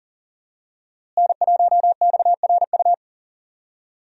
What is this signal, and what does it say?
Morse code, heard as a keyed tone, about 40 wpm, 700 Hz. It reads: N1XRU
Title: USA Call Signs - 5 Characters - Repeated 2x - 40wpm